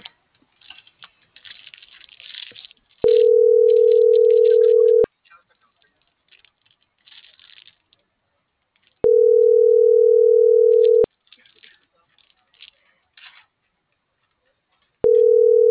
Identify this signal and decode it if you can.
Ringback tone (North America), 440 + 480 Hz, with speech.
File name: CENTRAL TERRESTRE